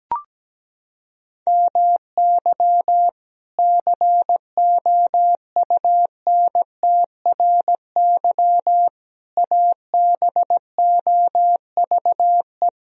Morse code, heard as a keyed tone, about 17 wpm, 700 Hz. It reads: MY COUNTRY ABOVE
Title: Sentences from Top 1000 Words - Review Part 2 - 17wpm